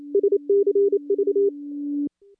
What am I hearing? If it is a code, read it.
Morse code, heard as a keyed tone, about 28 wpm, 400 Hz. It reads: SCV
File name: contests / CW-SS-k5tr- -2300